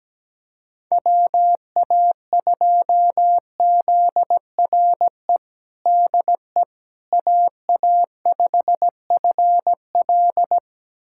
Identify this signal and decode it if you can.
Morse code, heard as a keyed tone, about 17 wpm, 700 Hz. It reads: WA2ZRE DE AA5FL
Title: USA Callsign DE Callsign 17wpm